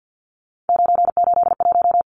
Morse code, heard as a keyed tone, about 48 wpm, 700 Hz. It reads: 981
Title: Numbers - 3-Digit 50wpm